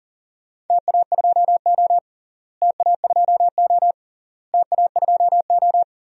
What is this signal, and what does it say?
Morse code, heard as a keyed tone, about 39 wpm, 700 Hz. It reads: TA2O TA2O TA2O
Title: DXCC-100 Call Signs - 4 Characters - Repeated 3x - 40wpm